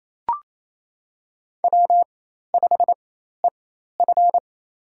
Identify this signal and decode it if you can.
Morse code, heard as a keyed tone, about 27 wpm, spaced as for 13 wpm, 700 Hz. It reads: W5EF
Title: USA Call Signs - 4x Character Spacing - 28wpm